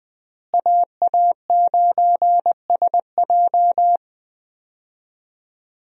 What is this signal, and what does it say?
Morse code, heard as a keyed tone, about 20 wpm, 700 Hz. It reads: AA9SJ
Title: USA Call Signs - 5 Characters - Repeated 2x - 20wpm